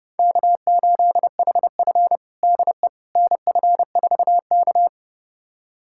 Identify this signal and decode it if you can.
Morse code, heard as a keyed tone, about 32 wpm, 700 Hz. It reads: K8HF DE NF4K
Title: USA Callsign DE Callsign - 4 Characters - 30wpm